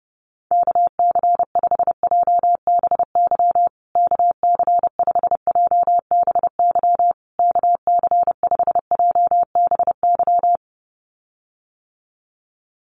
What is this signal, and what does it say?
Morse code, heard as a keyed tone, about 30 wpm, 700 Hz. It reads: KC5JBY KC5JBY KC5JBY
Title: USA Call Signs Repeated 3x 30wpm